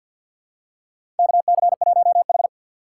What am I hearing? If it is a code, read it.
Morse code, heard as a keyed tone, about 53 wpm, 700 Hz. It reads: KC1H
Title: USA Call Signs - 4 Characters - 6x Character Spacing - 50wpm